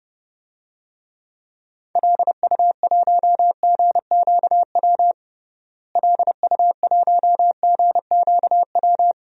Morse code, heard as a keyed tone, about 30 wpm, 700 Hz. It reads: LU1GQW LU1GQW
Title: DXCC-100 Call Signs - 6 Characters - Repeated 2x - 30wpm